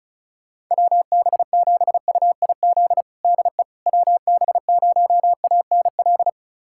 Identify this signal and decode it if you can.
Morse code, heard as a keyed tone, about 34 wpm, 700 Hz. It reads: WB7UIZ DE WB0ANL
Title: USA Callsign DE Callsign - 6 Characters - 35wpm